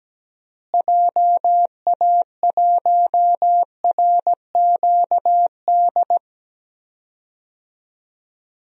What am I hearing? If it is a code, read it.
Morse code, heard as a keyed tone, about 17 wpm, 700 Hz. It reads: JA1RQD
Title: DXCC-100 Call Signs - 6 Characters - Repeated 3x - 17wpm